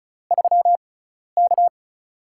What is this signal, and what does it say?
Morse code, heard as a keyed tone, about 34 wpm, spaced as for 12 wpm, 700 Hz. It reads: 3K
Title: USA Call Signs - 6x Character Spacing - 35wpm